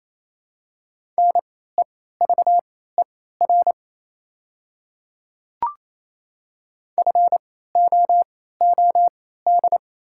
Morse code, heard as a keyed tone, about 28 wpm, spaced as for 16 wpm, 700 Hz. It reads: NEVER FOOD
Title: 3x Character Spacing - Top 500 Words - 28wpm